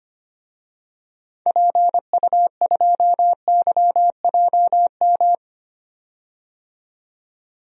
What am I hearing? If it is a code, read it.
Morse code, heard as a keyed tone, about 25 wpm, 700 Hz. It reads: PU2YJM
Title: DXCC-100 Call Signs - 6 Characters - 25wpm